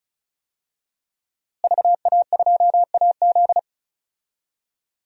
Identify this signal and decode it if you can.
Morse code, heard as a keyed tone, about 36 wpm, 700 Hz. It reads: VA2AZ
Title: DXCC-100 Call Signs - 5 Characters - Repeated 3x - 35wpm